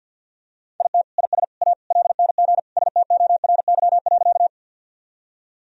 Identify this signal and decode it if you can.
Morse code, heard as a keyed tone, about 46 wpm, 700 Hz. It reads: IT IS A LNG STORY.
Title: Sentences from Top 300 Words and Abbreviations 50wpm